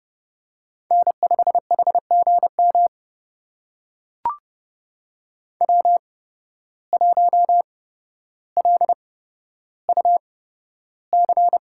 Morse code, heard as a keyed tone, about 32 wpm, spaced as for 13 wpm, 700 Hz. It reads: N5HGM W1LUC
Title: USA Call Signs - 5 Characters - 8x Character Spacing - 30wpm